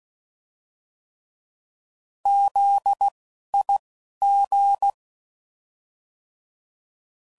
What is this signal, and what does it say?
Morse code, heard as a keyed tone, about 16 wpm, spaced as for 11 wpm, 800 Hz. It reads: ZIG
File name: Prefixes / By First Letter / Zs